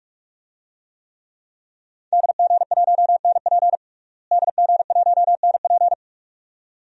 Morse code, heard as a keyed tone, about 46 wpm, 700 Hz. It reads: DG1NP DG1NP
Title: DXCC-100 Call Signs - 5 Characters - Repeated 2x - 45wpm